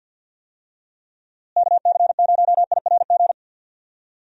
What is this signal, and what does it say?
Morse code, heard as a keyed tone, about 51 wpm, 700 Hz. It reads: KC0IRG
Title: USA Call Signs - 6 Characters - Repeated 2x - 50wpm